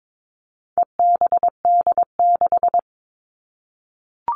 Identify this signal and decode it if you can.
Morse code, heard as a keyed tone, about 22 wpm, 700 Hz. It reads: EBD6
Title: All Cut Numbers - 4-Digit 22wpm